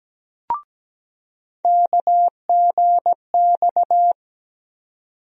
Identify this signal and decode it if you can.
Morse code, heard as a keyed tone, about 17 wpm, 700 Hz. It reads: KGX
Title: USA Call Sign Postfixes - 3 Letters 17wpm